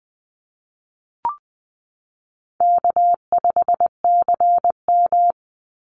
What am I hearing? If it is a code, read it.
Morse code, heard as a keyed tone, about 20 wpm, 700 Hz. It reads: K5CM
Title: Easy Call Signs Repeated 2x - 20wpm